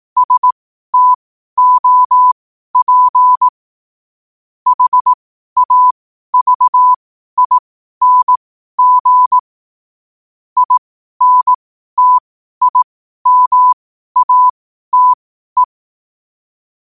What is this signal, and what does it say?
Morse code, heard as a keyed tone, about 18 wpm, spaced as for 12 wpm, 1000 Hz. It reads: STOP HAVING INTIMATE
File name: qotdfiles / 12wpm